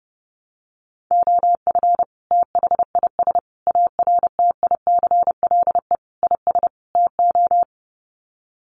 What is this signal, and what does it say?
Morse code, heard as a keyed tone, about 28 wpm, 700 Hz. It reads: OF THIS ARTICLE IS TO
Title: Sets of 5 Words Encore2 30wpm